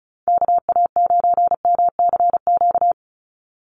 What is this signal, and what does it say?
Morse code, heard as a keyed tone, about 34 wpm, 700 Hz. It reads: KA9MCQ
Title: USA Call Signs 35wpm